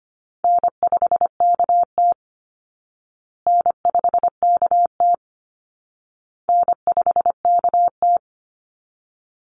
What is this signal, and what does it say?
Morse code, heard as a keyed tone, about 25 wpm, 700 Hz. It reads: N5KT N5KT N5KT
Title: Easy Call Signs Repeated 3x - 25wpm